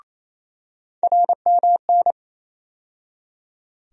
Morse code, heard as a keyed tone, about 28 wpm, 700 Hz. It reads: RMN
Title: USA Call Sign Postfixes - 3 Letters 28wpm